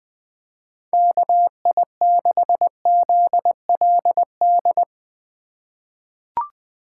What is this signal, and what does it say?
Morse code, heard as a keyed tone, about 20 wpm, 700 Hz. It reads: KI6ZLD
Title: USA Call Signs - 6 Characters - 3x Character Spacing - 20wpm